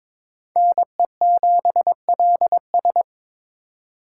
Morse code, heard as a keyed tone, about 26 wpm, 700 Hz. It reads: NE7LS
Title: Speed Racing - USA 2x2 Call Signs 22wpm